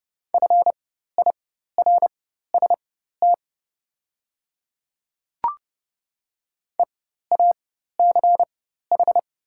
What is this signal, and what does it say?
Morse code, heard as a keyed tone, about 30 wpm, spaced as for 14 wpm, 700 Hz. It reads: FIRST EACH
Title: 4x Character Spacing - Top 100 Words - 30wpm